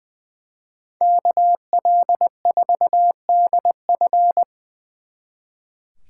Morse code, heard as a keyed tone, about 23 wpm, 700 Hz. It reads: KL4DF
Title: Speed Racing - USA Call Signs 20wpm